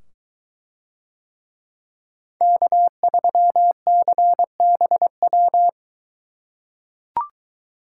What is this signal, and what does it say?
Morse code, heard as a keyed tone, about 23 wpm, 700 Hz. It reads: K3CBW